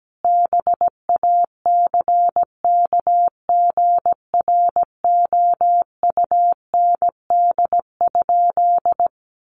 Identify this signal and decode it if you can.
Morse code, heard as a keyed tone, about 17 wpm, 700 Hz. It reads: BACKGROUND?
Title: Sentences from Top 1300 Words 17wpm